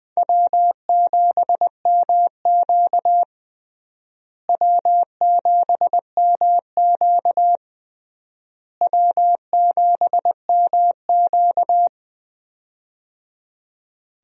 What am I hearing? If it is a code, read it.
Morse code, heard as a keyed tone, about 20 wpm, 700 Hz. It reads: W7MQ W7MQ W7MQ
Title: USA Call Signs - 4 Characters - Repeated 3x - 20wpm